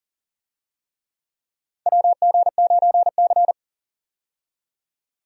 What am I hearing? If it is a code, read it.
Morse code, heard as a keyed tone, about 40 wpm, 700 Hz. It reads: WG9C
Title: USA Call Signs - 4 Characters - 40wpm